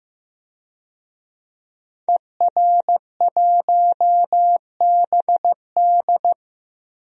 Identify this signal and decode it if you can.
Morse code, heard as a keyed tone, about 15 wpm, 700 Hz. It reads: ER1BD
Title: DXCC-100 Call Signs - 5 Characters - Repeated 3x - 15wpm